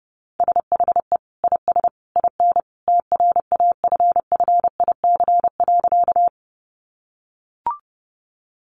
Morse code, heard as a keyed tone, about 30 wpm, 700 Hz. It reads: SHE IS IN TRAFFIC.
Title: Sentences from Top 1300 Words 30wpm